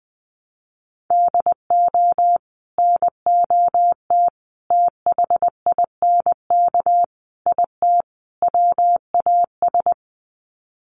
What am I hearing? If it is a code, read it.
Morse code, heard as a keyed tone, about 20 wpm, 700 Hz. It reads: DO NOT THINK IT WAS
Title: Sets of 5 Words Encore 20wpm